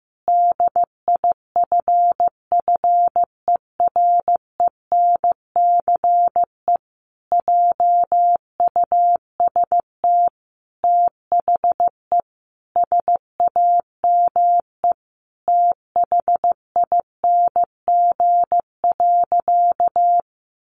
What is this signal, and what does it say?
Morse code, heard as a keyed tone, about 15 wpm, 700 Hz. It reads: DIFFERENCE JUST THE SAME THING.
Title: Sentences from Top 800 Words 15wpm